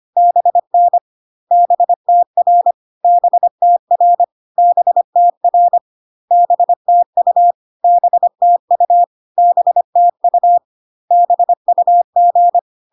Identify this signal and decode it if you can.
Morse code, heard as a keyed tone, about 25 wpm, 700 Hz. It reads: BN BTR BTR BTR BTU BTU BTU BUG
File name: CW abbreviations alphabetical repeated 3 times 25 25wpm